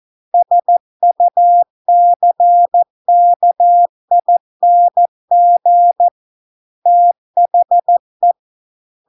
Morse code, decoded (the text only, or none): SUCKING THE